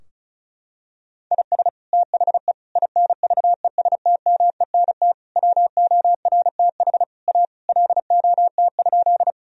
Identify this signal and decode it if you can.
Morse code, heard as a keyed tone, about 36 wpm, 700 Hz. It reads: IS THE INVESTMENT WORTH A LOT?
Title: Sentences from Top 1000 Words - Review Part 4 - 35wpm